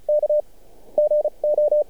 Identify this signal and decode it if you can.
Morse code, heard as a keyed tone, about 34 wpm, spaced as for 29 wpm, 600 Hz. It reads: K GO
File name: archived / CRSnet